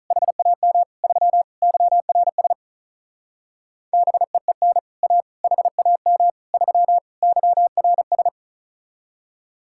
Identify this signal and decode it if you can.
Morse code, heard as a keyed tone, about 45 wpm, 700 Hz. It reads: HAM 3 YRS BEEN A HAM 3 YRS